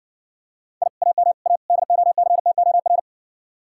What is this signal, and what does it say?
Morse code, heard as a keyed tone, about 60 wpm, 700 Hz. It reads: I AM A DOCTOR